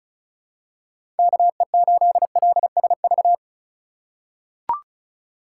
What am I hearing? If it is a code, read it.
Morse code, heard as a keyed tone, about 34 wpm, 700 Hz. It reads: KE8LSV
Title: USA Call Signs - 3x Character Spacing - 35wpm